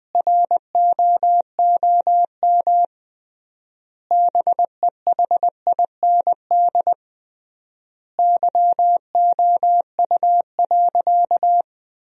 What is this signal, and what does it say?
Morse code, heard as a keyed tone, about 20 wpm, 700 Hz. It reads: ROOM BEHIND YOU.
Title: 3x Word-Spacing - Sentences from Top 500 Words - 20wpm